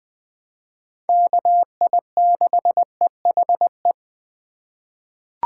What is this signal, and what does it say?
Morse code, heard as a keyed tone, about 20 wpm, 700 Hz. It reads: KI6EHE
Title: USA Call Signs - 6 Characters - 3x Character Spacing - 20wpm